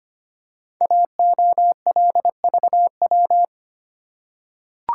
Morse code, heard as a keyed tone, about 25 wpm, 690 Hz. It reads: AOLVW
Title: Code Groups - 5 Letters - 25wpm